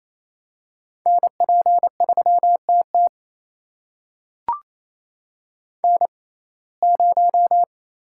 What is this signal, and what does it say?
Morse code, heard as a keyed tone, about 28 wpm, spaced as for 20 wpm, 700 Hz. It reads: NP3TT N0
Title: USA Call Signs - 5 Characters - 6x Character Spacing - 28wpm